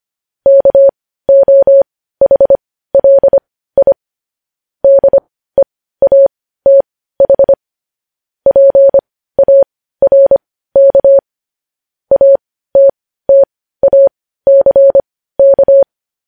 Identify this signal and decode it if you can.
Morse code, heard as a keyed tone, about 25 wpm, spaced as for 15 wpm, 550 Hz. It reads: KOHLI DEATH PARK ATTACK
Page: News Headlines in Morse Code at 15 WPM